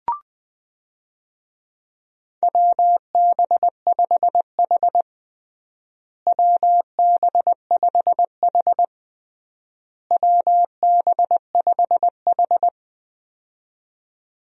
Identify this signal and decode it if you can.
Morse code, heard as a keyed tone, about 20 wpm, 700 Hz. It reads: WB5H WB5H WB5H
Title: USA Call Signs - 4 Characters - Repeated 3x - 20wpm